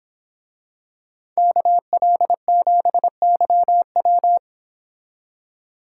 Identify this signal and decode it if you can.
Morse code, heard as a keyed tone, about 26 wpm, 700 Hz. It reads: KL7YW
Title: Speed Racing - USA Call Signs 22wpm